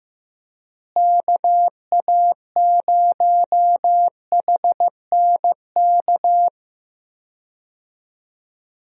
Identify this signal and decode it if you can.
Morse code, heard as a keyed tone, about 16 wpm, 700 Hz. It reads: KA0HNK
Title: USA Call Signs - 6 Characters - Repeated 3x - 15wpm